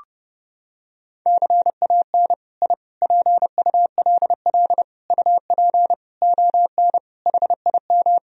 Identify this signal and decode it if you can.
Morse code, heard as a keyed tone, about 30 wpm, 700 Hz. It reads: CAN I PULL UP ON HIM
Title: Sentences from Top 1000 Words - Review Part 2 - 30wpm